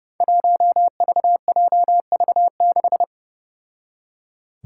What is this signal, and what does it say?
Morse code, heard as a keyed tone, about 30 wpm, 700 Hz. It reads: 1VJV6
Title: Five Challenging Letter-Number Combinations 30wpm